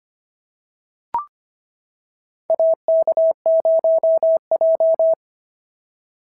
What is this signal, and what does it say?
Morse code, heard as a keyed tone, about 25 wpm, 690 Hz. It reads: AK0J
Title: SOTA QSO Practice 25wpm